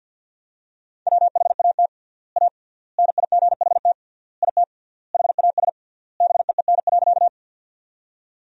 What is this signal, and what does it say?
Morse code, heard as a keyed tone, about 53 wpm, 700 Hz. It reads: WHAT A NIGHT IT HAS BEEN.
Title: 3x Word-Spacing - Sentences from Top 300 Words - 50wpm